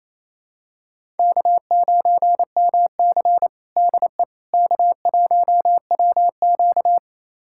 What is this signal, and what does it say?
Morse code, heard as a keyed tone, about 28 wpm, 700 Hz. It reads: K9MC DE K1WQ
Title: USA Callsign DE Callsign - 4 Characters - 28wpm